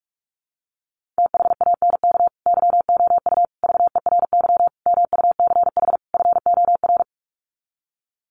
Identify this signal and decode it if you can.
Morse code, heard as a keyed tone, about 44 wpm, 700 Hz. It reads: THANK YOU VERY MUCH FOR
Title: Sets of 5 Words 45wpm